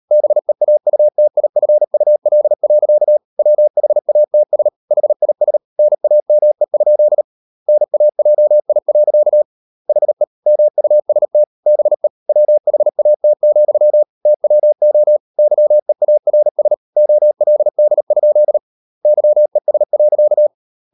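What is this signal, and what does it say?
Morse code, heard as a keyed tone, about 38 wpm, 600 Hz. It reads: BEAUTIFUL. WHATS HIS NAME? NAJI. HE MUST BE WHAT, TWO YEARS OLD? YES.
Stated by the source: Morse code practice at 38 WPM, sending Ken Follett's novel Never in English, from CW med Gnister.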